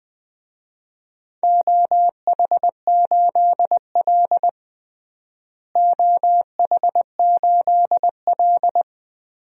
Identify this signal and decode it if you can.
Morse code, heard as a keyed tone, about 20 wpm, 700 Hz. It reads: OH8L OH8L
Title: DXCC-100 Call Signs - 4 Characters - Repeated 2x - 20wpm